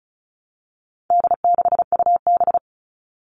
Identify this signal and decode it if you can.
Morse code, heard as a keyed tone, about 34 wpm, 700 Hz. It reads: D6UB
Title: All Cut Numbers - 4-Digit 35wpm